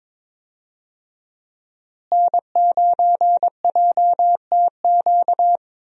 Morse code, heard as a keyed tone, about 22 wpm, 700 Hz. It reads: N9JTQ
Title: USA Call Signs - 5 Characters - Repeated 3x - 22wpm